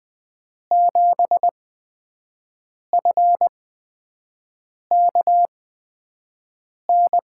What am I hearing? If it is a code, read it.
Morse code, heard as a keyed tone, about 20 wpm, spaced as for 5 wpm, 700 Hz. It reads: 7FKN
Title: USA Call Signs - 8x Character Spacing - 20wpm